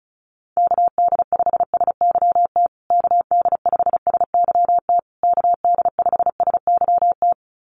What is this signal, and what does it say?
Morse code, heard as a keyed tone, about 34 wpm, 700 Hz. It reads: KD5SYT KD5SYT KD5SYT
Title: USA Call Signs Repeated 3x 35wpm